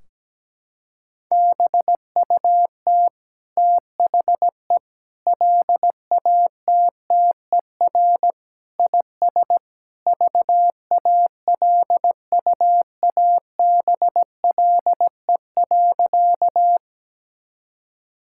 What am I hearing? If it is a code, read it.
Morse code, heard as a keyed tone, about 17 wpm, 700 Hz. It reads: BUT THE LATTER IS VALUABLE.
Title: Sentences from Top 1600 Words 17wpm